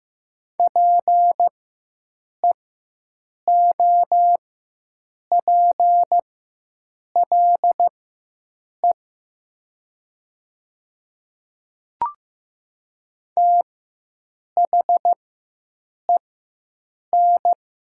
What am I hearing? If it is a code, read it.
Morse code, heard as a keyed tone, about 15 wpm, spaced as for 7 wpm, 700 Hz. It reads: PEOPLE THEN
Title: 4x Character Spacing - Top 200 Words - 15wpm